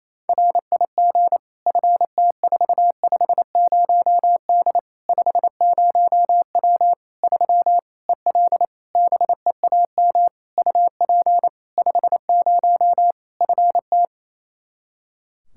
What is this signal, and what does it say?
Morse code, heard as a keyed tone, about 28 wpm, 700 Hz. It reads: RIG FT450D 50W 3 EL BEAM UP 50 FT
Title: K1USN SST QSO Practice 28wpm